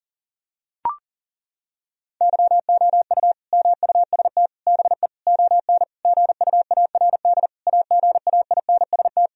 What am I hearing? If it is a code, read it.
Morse code, heard as a keyed tone, about 40 wpm, 700 Hz. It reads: YOU MUST BE ON GUARD AGAINST
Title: Sentences from Top 1000 Words - Review Part 4 - 40wpm